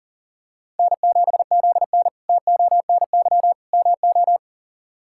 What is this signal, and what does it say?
Morse code, heard as a keyed tone, about 40 wpm, 700 Hz. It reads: N7ZN TONY MO
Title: K1USN SST QSO Practice - 40wpm